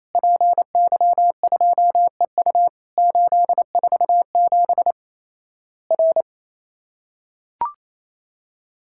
Morse code, heard as a keyed tone, about 28 wpm, 700 Hz. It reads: PY2EU 847 R
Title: Worked All Europe DX Contest - QTC Exchange - 28wpm